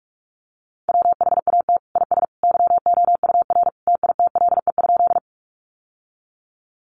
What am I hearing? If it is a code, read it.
Morse code, heard as a keyed tone, about 44 wpm, 700 Hz. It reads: WHAT IS YOUR TITLE?
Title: Sentences from Top 1200 Words 45wpm